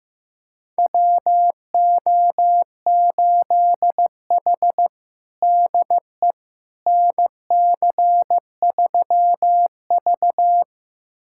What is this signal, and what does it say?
Morse code, heard as a keyed tone, about 16 wpm, 700 Hz. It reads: WO8H DE NC3V
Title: USA Callsign DE Callsign - 4 Characters - 15wpm